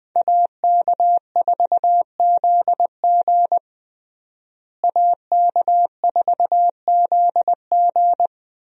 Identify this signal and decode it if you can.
Morse code, heard as a keyed tone, about 20 wpm, 700 Hz. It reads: AK4ZG AK4ZG
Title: USA Call Signs - 5 Characters - Repeated 2x - 20wpm